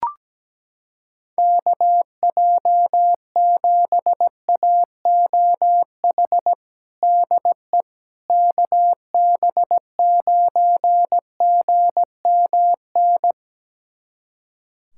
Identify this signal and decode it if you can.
Morse code, heard as a keyed tone, about 17 wpm, 700 Hz. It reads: KJ7AOH DE KB9GMN
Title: USA Callsign DE Callsign - 6 Characters - 17wpm